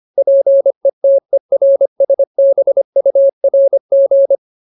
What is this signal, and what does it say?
Morse code, heard as a keyed tone, about 25 wpm, 550 Hz. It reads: PETERSBURG
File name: Ultimate Word List / 25WPM Ultimate